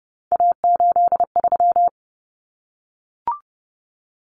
Morse code, heard as a keyed tone, about 30 wpm, 700 Hz. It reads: A83
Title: Common Cut Numbers - 3-Digit 30wpm